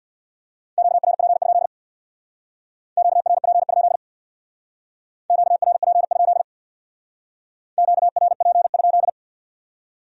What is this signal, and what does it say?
Morse code, heard as a keyed tone, about 66 wpm, 700 Hz. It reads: QRP? QRP? QRP? QRP?